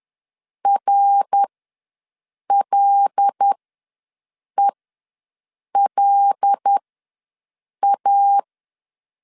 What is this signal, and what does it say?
Morse code, heard as a keyed tone, about 10 wpm, spaced as for 6 wpm, 800 Hz. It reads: RLELA
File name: LEZIONI / 5 LBVC / 5 L ws 10wpm 3636